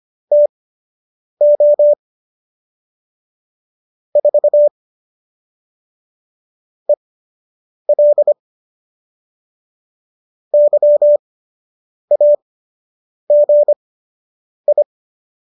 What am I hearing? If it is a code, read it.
Morse code, heard as a keyed tone, about 25 wpm, spaced as for 8 wpm, 590 Hz. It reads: TO 4 EL YAGI